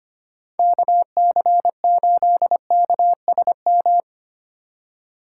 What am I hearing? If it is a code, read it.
Morse code, heard as a keyed tone, about 32 wpm, 700 Hz. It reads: KC8KSM